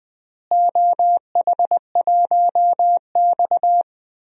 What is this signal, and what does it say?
Morse code, heard as a keyed tone, about 20 wpm, 700 Hz. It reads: OH1X
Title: DXCC-100 Call Signs - 4 Characters - Repeated 3x - 20wpm